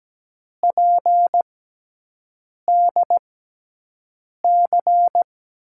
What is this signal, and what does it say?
Morse code, heard as a keyed tone, about 17 wpm, spaced as for 6 wpm, 700 Hz. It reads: PDC
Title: USA Call Signs - 6 Characters - 6x Character Spacing - 17wpm